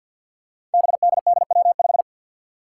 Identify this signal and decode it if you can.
Morse code, heard as a keyed tone, about 53 wpm, 700 Hz. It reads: BDDW5
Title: Five Challenging Letter-Number Combinations 50wpm